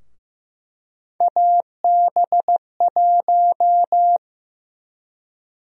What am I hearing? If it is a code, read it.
Morse code, heard as a keyed tone, about 15 wpm, 700 Hz. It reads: AB1
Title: USA Call Sign Prefixes - 15wpm